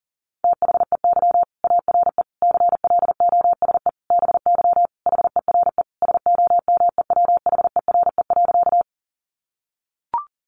Morse code, heard as a keyed tone, about 40 wpm, 700 Hz. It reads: THEY ARE CLOSE BY HERE SOMEWHERE.
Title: Sentences from Top 1400 Words 40wpm